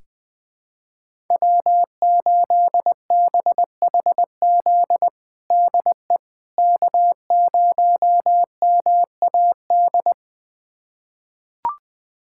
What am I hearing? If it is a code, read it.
Morse code, heard as a keyed tone, about 20 wpm, 700 Hz. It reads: W8BHZ DE K0MAD
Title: USA Callsign DE Callsign - 5 Characters - 20wpm